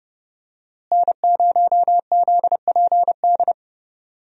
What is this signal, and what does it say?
Morse code, heard as a keyed tone, about 30 wpm, 700 Hz. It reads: N0ZPD
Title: USA Call Signs - 5 Characters - Repeated 3x - 30wpm